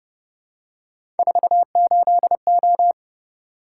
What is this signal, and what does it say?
Morse code, heard as a keyed tone, about 29 wpm, 700 Hz. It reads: 48O